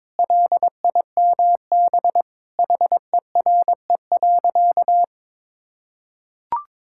Morse code, heard as a keyed tone, about 23 wpm, 700 Hz. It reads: LIMB HERE.
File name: practice / 22wpm / Sentences with 8-Word Phrases - 22wpm